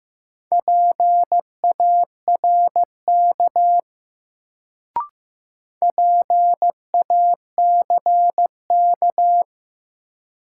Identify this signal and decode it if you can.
Morse code, heard as a keyed tone, about 15 wpm, 700 Hz. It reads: PARK PACK
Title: Sentences from Top 1300 Words - New Word Drill - 15wpm